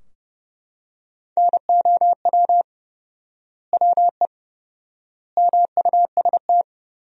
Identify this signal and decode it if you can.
Morse code, heard as a keyed tone, about 30 wpm, 700 Hz. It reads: NOW WE MUST
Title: 4x Word-Spacing - Sentences from Top 500 Words - 30wpm